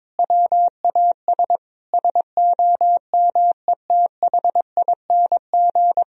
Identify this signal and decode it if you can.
Morse code, heard as a keyed tone, about 22 wpm, 700 Hz. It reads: WAS SOMETHING
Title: Sentences from Top 1000 Words - Review Part 2 - 22wpm